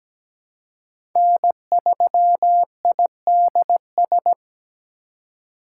Morse code, heard as a keyed tone, about 17 wpm, 700 Hz. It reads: N3IDS
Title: USA Call Signs - 5 Characters - 3x Character Spacing - 17wpm